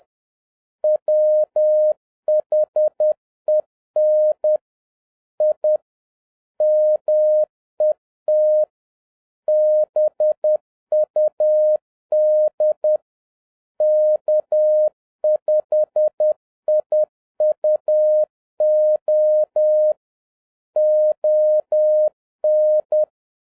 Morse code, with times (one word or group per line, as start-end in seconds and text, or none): 0.8-4.6 WHEN
5.4-5.8 I
6.6-8.6 MET
9.5-13.0 BUD
13.8-19.9 K5IUO
20.8-23.0 ON